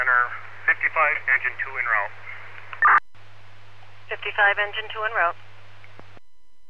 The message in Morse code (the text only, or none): none